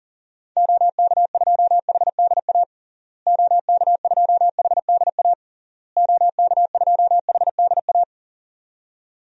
Morse code, heard as a keyed tone, about 37 wpm, 700 Hz. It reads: OK2HDA OK2HDA OK2HDA
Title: DXCC-100 Call Signs - 6 Characters - Repeated 3x - 40wpm